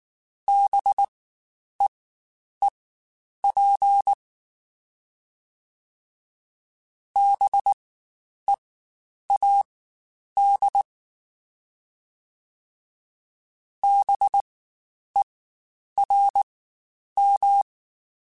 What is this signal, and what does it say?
Morse code, heard as a keyed tone, about 19 wpm, spaced as for 9 wpm, 800 Hz. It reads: BEEP BEAD BERM
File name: Spaced Alphas / Spaced Alphas mp3s / Bs sp @18